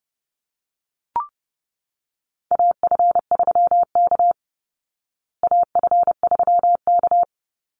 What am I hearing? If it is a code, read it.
Morse code, heard as a keyed tone, about 30 wpm, 700 Hz. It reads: AF3K AF3K
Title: Call Signs Repeated 2x - 30wpm